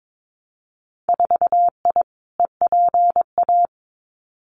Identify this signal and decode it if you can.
Morse code, heard as a keyed tone, about 22 wpm, 700 Hz. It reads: 4I EPA
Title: Winter Field Day Exchanges Encore 22wpm